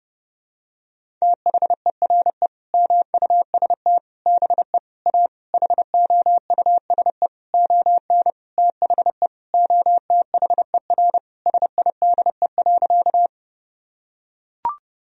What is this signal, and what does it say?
Morse code, heard as a keyed tone, about 32 wpm, 700 Hz. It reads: THERE MUST BE A HOUSE ON THE OTHER SIDE.
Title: Sentences from Top 1000 Words - Review Part 1 - 30wpm